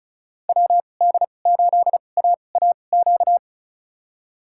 Morse code, heard as a keyed tone, about 36 wpm, spaced as for 25 wpm, 700 Hz. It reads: WD8AAQ
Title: USA Call Signs - 2x Character Spacing - 35wpm